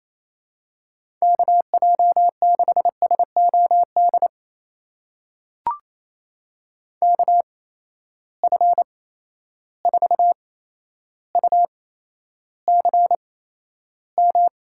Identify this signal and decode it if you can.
Morse code, heard as a keyed tone, about 28 wpm, spaced as for 12 wpm, 700 Hz. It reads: KJ6SOD KF4UCM
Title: USA Call Signs - 6 Characters - 8x Character Spacing - 28wpm